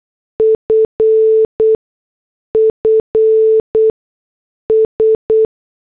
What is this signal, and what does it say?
Morse code, heard as a keyed tone, about 8 wpm, 440 Hz. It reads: F F S
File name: morse / headlines